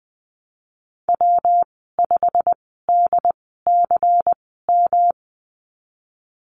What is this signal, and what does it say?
Morse code, heard as a keyed tone, about 20 wpm, spaced as for 15 wpm, 700 Hz. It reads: W5DCM
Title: 140 Lesson 23g - Letter f - Call Signs 20wpm